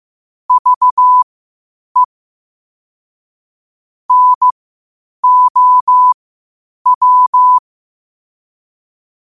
Morse code, heard as a keyed tone, about 15 wpm, spaced as for 8 wpm, 1000 Hz. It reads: VE NOW
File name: qotdfiles / 8wpm